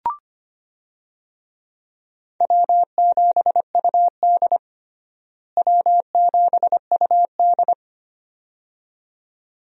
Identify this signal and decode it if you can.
Morse code, heard as a keyed tone, about 25 wpm, 700 Hz. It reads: W7UD W7UD
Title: USA Call Signs - 4 Characters - Repeated 2x - 25wpm